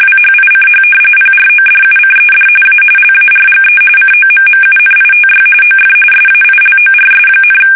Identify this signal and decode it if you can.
RTTY, about 45.45 baud. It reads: WHICH PRODUCED A LARGE COMMUNITY OF SAILORS,
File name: fsk850 / usnavy1